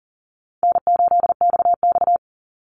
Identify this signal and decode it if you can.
Morse code, heard as a keyed tone, about 42 wpm, 700 Hz. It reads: N8XX
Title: Call Signs - 40wpm